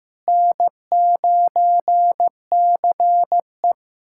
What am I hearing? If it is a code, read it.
Morse code, heard as a keyed tone, about 15 wpm, 700 Hz. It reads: N9CE
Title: USA Callsign DE Callsign - 4 Characters - 15wpm